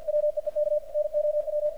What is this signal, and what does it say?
Morse code, heard as a keyed tone, about 52 wpm, 600 Hz, with radio noise. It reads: WEEK NOW
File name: archived / CRSnet